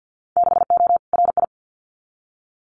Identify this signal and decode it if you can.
Morse code, heard as a keyed tone, about 50 wpm, 700 Hz. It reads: 6O RI
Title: Winter Field Day Exchanges Encore 50wpm